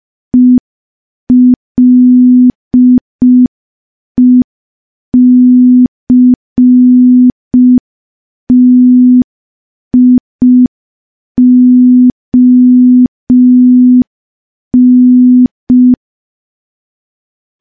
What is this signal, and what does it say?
Morse code, heard as a keyed tone, about 5 wpm, 250 Hz. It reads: ELECTION
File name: morse / morse-training-05WPM